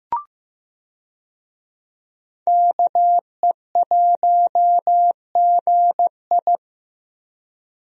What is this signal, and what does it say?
Morse code, heard as a keyed tone, about 15 wpm, 700 Hz. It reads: KE1GI
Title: USA Call Signs - 5 Characters - Repeated 2x - 15wpm